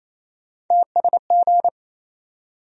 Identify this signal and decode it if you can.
Morse code, heard as a keyed tone, about 28 wpm, 700 Hz. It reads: TSG
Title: USA Call Sign Postfixes - 3 Letters 28wpm